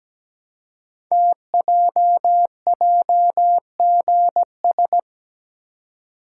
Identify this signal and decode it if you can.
Morse code, heard as a keyed tone, about 17 wpm, 700 Hz. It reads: TJJGS
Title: Code Groups - 5 Letters - 17wpm